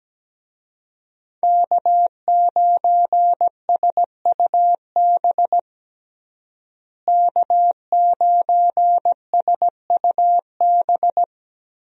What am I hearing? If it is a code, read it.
Morse code, heard as a keyed tone, about 17 wpm, 700 Hz. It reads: K9SUB K9SUB
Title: USA Call Signs - 5 Characters - Repeated 2x - 17wpm